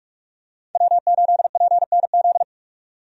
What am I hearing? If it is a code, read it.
Morse code, heard as a keyed tone, about 46 wpm, 700 Hz. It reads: W8PNZ
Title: USA Call Signs - 5 Characters - 45wpm